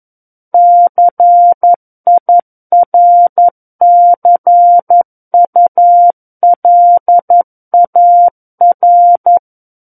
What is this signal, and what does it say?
Morse code, heard as a keyed tone, about 11 wpm, 700 Hz. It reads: CIRCULAR
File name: 11 WPM